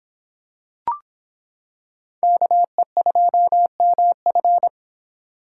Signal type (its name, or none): Morse code